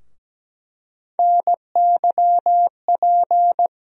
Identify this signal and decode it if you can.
Morse code, heard as a keyed tone, about 17 wpm, 700 Hz. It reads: NYP
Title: USA Call Sign Postfixes - 3 Letters 17wpm